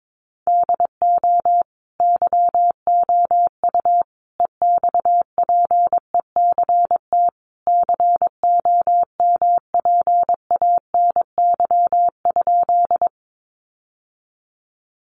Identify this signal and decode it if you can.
Morse code, heard as a keyed tone, about 22 wpm, 700 Hz. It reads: DO YOU EXPECT COMPANY?
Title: Sentences from Top 1100 Words 22wpm